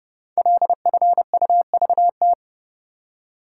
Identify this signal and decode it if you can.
Morse code, heard as a keyed tone, about 30 wpm, 690 Hz. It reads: LFUVT
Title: Code Groups - 5 Letters - 30wpm